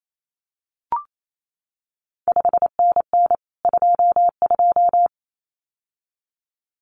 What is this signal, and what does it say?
Morse code, heard as a keyed tone, about 28 wpm, 700 Hz. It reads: 5NN 22
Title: CQ WW DX Contest Exchange Encore - 28wpm